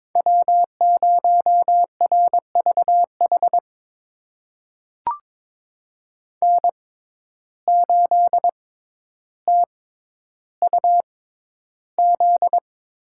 Morse code, heard as a keyed tone, about 21 wpm, spaced as for 11 wpm, 700 Hz. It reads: W0RVH N8TUZ
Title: USA Call Signs - 5 Characters - 6x Character Spacing - 22wpm